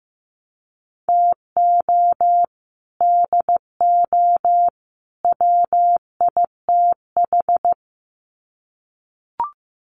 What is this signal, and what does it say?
Morse code, heard as a keyed tone, about 15 wpm, 700 Hz. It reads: TO DO WITH
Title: Sets of 3 Words 15wpm